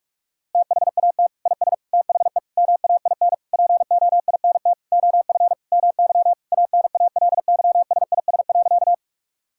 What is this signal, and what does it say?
Morse code, heard as a keyed tone, about 46 wpm, 700 Hz. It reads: THAT IS THE MAIN POINT OF MY ANALYSIS.
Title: Sentences from Top 1000 Words - Review Part 5 - 45wpm